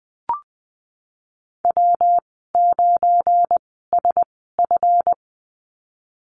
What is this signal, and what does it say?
Morse code, heard as a keyed tone, about 20 wpm, spaced as for 15 wpm, 700 Hz. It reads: W9SF
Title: 186 Lesson 30g - Letter b - Call Signs 20wpm